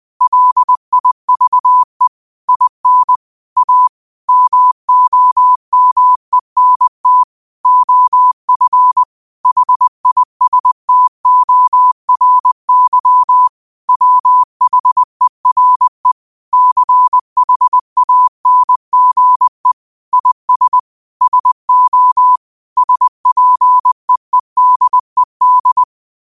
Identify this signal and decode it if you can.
Morse code, heard as a keyed tone, about 20 wpm, 1000 Hz. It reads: LIVE IN A MOMENT OF HISTORY WHERE CHANGE IS SO SPEEDED
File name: qotdfiles / 20wpm